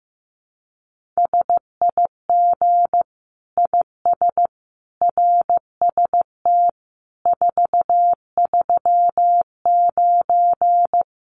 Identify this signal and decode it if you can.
Morse code, heard as a keyed tone, about 15 wpm, 700 Hz. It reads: SIG IS RST 439
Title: Common QSO Phrases 15wpm